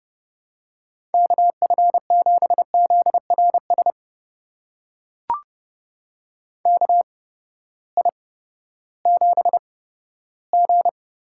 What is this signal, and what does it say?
Morse code, heard as a keyed tone, about 30 wpm, spaced as for 15 wpm, 700 Hz. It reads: KF7ZRS KI7G
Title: USA Call Signs - 6 Characters - 8x Character Spacing - 30wpm